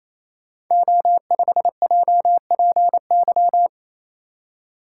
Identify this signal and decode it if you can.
Morse code, heard as a keyed tone, about 27 wpm, 700 Hz. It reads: O5JPY
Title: Code Groups - 5 Letter:Numbers - 28wpm